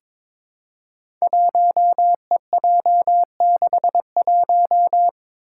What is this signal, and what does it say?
Morse code, heard as a keyed tone, about 22 wpm, 700 Hz. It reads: 1EJ61
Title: Five Challenging Letter-Number Combinations 22wpm